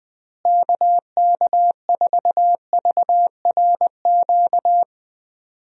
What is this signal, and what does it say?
Morse code, heard as a keyed tone, about 20 wpm, 700 Hz. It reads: KK4VRQ
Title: USA Call Signs - 6 Characters - Repeated 3x - 20wpm